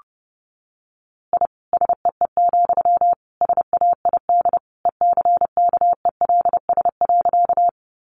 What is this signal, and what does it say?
Morse code, heard as a keyed tone, about 30 wpm, 700 Hz. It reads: I SEE, SAID ECKELS.
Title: A Sound Of Thunder 30wpm